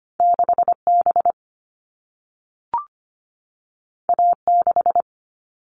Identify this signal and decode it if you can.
Morse code, heard as a keyed tone, about 25 wpm, 700 Hz. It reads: 6B A6
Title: All Cut Numbers - 2-Digit 25wpm